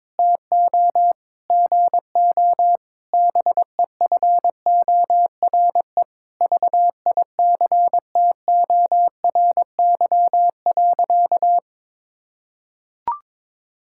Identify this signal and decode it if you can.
Morse code, heard as a keyed tone, about 22 wpm, 700 Hz. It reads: TO GO BEFORE VICTORY.
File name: Sentences with 9-Word Phrases - 22wpm